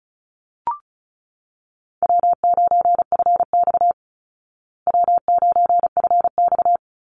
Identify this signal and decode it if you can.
Morse code, heard as a keyed tone, about 34 wpm, 700 Hz. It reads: W9FX W9FX
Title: Easy Call Signs Repeated 2x - 35wpm